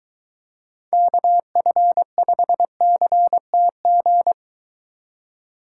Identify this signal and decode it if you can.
Morse code, heard as a keyed tone, about 23 wpm, 700 Hz. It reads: KF5CTG